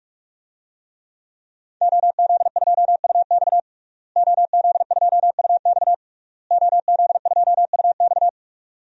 Morse code, heard as a keyed tone, about 46 wpm, 700 Hz. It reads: OZ2UX OZ2UX OZ2UX
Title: DXCC-100 Call Signs - 5 Characters - Repeated 3x - 45wpm